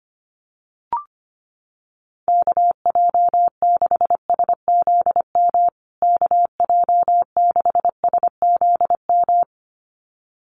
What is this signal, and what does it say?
Morse code, heard as a keyed tone, about 25 wpm, 700 Hz. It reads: KJ6SZM KJ6SZM
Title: USA Call Signs Repeated 2x 25wpm